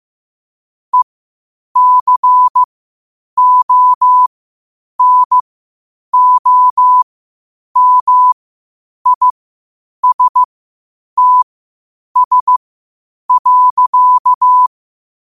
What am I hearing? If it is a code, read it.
Morse code, heard as a keyed tone, about 15 wpm, spaced as for 8 wpm, 1000 Hz. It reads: ECONOMISTS.
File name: qotdfiles / 8wpm